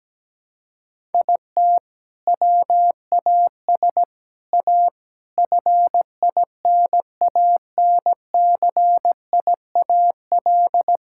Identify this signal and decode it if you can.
Morse code, heard as a keyed tone, about 17 wpm, 700 Hz. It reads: IT WAS A FINANCIAL
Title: Sentences from Top 1000 Words - Review Part 6 - 17wpm